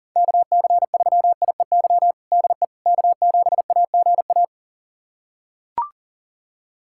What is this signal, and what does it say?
Morse code, heard as a keyed tone, about 40 wpm, 700 Hz. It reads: KC3IEY DE K7AGA
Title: USA Callsign DE Callsign 40wpm